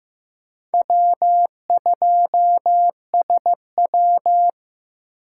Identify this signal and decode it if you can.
Morse code, heard as a keyed tone, about 15 wpm, 700 Hz. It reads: W2SW
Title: USA Call Signs - 4 Characters - 2x Character Spacing - 15wpm